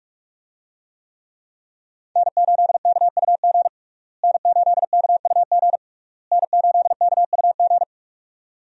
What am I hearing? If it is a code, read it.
Morse code, heard as a keyed tone, about 44 wpm, 700 Hz. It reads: N8KUG N8KUG N8KUG
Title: USA Call Signs - 5 Characters - Repeated 3x - 45wpm